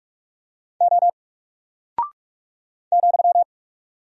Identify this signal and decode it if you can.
Morse code, heard as a keyed tone, about 44 wpm, 700 Hz. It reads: O ,
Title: Single Character (Letter:Number:Punctuation:Prosign) 45wpm